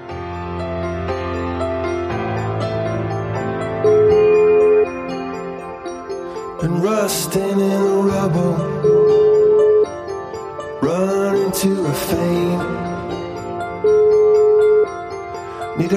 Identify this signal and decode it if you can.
Ringback tone (Europe, 425 Hz), with speech.